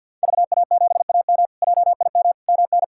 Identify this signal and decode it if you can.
Morse code, heard as a keyed tone, about 52 wpm, 690 Hz. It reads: VA7AM JIM MN
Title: K1USN SST QSO Practice - 50wpm